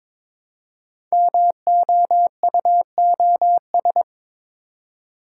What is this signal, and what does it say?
Morse code, heard as a keyed tone, about 22 wpm, 700 Hz. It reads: MOUOS
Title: Code Groups - 5 Letters - 22wpm